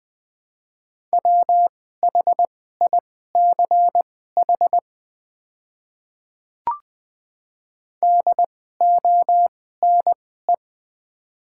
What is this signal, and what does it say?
Morse code, heard as a keyed tone, about 20 wpm, spaced as for 14 wpm, 690 Hz. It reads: WHICH DONE
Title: 2x Character Spacing - Top 500 Words - 20wpm